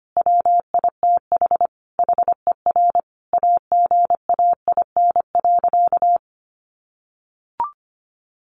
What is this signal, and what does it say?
Morse code, heard as a keyed tone, about 25 wpm, 700 Hz. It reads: WITH HER AGAIN.
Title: Sentences from Top 1500 Words 25wpm